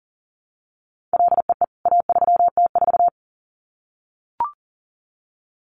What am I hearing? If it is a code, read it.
Morse code, heard as a keyed tone, about 40 wpm, 700 Hz. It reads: LEE A3T4
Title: CWT Exchange 40wpm